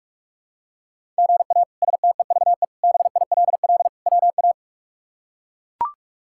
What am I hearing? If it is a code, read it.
Morse code, heard as a keyed tone, about 42 wpm, 700 Hz. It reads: GA STEVE BILL WA
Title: K1USN SST QSO Practice - 45wpm